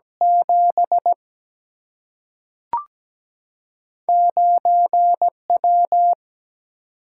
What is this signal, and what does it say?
Morse code, heard as a keyed tone, about 17 wpm, 700 Hz. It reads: 7 9W
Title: Single-Letter Single-Number 17wpm